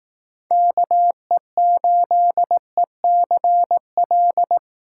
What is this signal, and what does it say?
Morse code, heard as a keyed tone, about 18 wpm, 700 Hz. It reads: KE8ECL
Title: Speed Racing - USA 2x3 Call Signs 15wpm